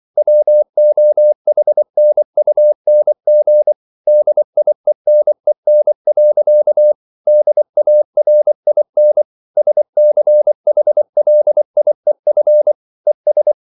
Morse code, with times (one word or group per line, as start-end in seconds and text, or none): 0.2-3.7 WOHNUNG
4.1-6.9 DIENEN.
7.3-9.2 DARIN
9.6-12.7 SCHLIEF
13.1-13.5 ES